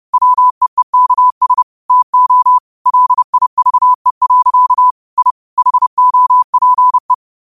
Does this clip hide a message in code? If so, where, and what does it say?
Yes: Morse code from 0.1 to 7.1 s, heard as a keyed tone, about 30 wpm, 1000 Hz. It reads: WEEKS TO LIVE. I HOPE